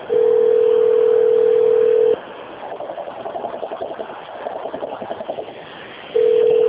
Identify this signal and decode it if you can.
Ringback tone (North America), 440 + 480 Hz.